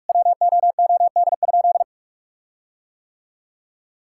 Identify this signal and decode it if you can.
Morse code, heard as a keyed tone, about 41 wpm, 700 Hz. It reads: WOOD?
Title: Sentences from Top 1600 Words 45wpm